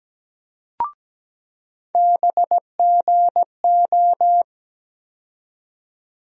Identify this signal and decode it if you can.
Morse code, heard as a keyed tone, about 17 wpm, 700 Hz. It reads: BGO
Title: USA Call Sign Postfixes - 3 Letters 17wpm